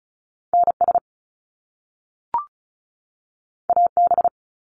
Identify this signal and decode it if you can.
Morse code, heard as a keyed tone, about 36 wpm, 700 Hz. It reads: NS AB
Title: Canadian Provinces and Territories 35wpm